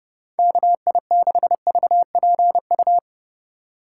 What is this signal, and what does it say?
Morse code, heard as a keyed tone, about 30 wpm, 700 Hz. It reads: KI6VPU